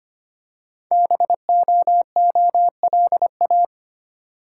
Morse code, heard as a keyed tone, about 25 wpm, 700 Hz. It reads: BOOLA
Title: Code Groups - 5 Letters - 25wpm